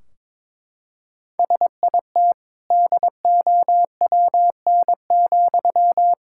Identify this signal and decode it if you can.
Morse code, heard as a keyed tone, about 22 wpm, 700 Hz. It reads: SIT DOWN,
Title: Sentences from Top 1600 Words 22wpm